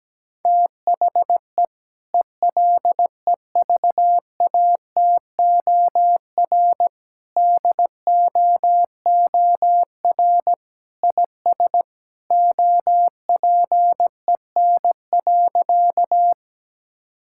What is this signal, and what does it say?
Morse code, heard as a keyed tone, about 17 wpm, 700 Hz. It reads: THE ELEVATOR DOOR IS OPEN.
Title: Sentences from Top 1600 Words 17wpm